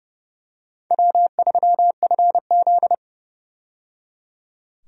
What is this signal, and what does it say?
Morse code, heard as a keyed tone, about 30 wpm, 700 Hz. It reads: W3FZ
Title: USA Call Signs - 4 Characters - 30wpm